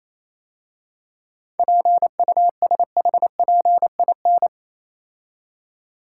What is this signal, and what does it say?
Morse code, heard as a keyed tone, about 27 wpm, 690 Hz. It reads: PUSHPIN
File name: Easy Words - Speed-Racing with Hints - 22wpm